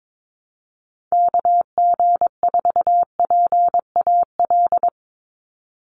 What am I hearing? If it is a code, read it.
Morse code, heard as a keyed tone, about 22 wpm, 700 Hz. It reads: KG4PAL
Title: USA Call Signs 22wpm